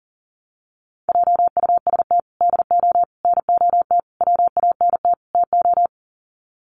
Morse code, heard as a keyed tone, about 40 wpm, 700 Hz. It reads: JUST DO NOT WANT TO
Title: Sets of 5 Words Encore 40wpm